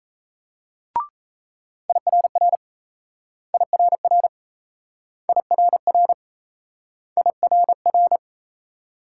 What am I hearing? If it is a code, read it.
Morse code, heard as a keyed tone, about 38 wpm, 700 Hz. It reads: IRR IRR IRR IRR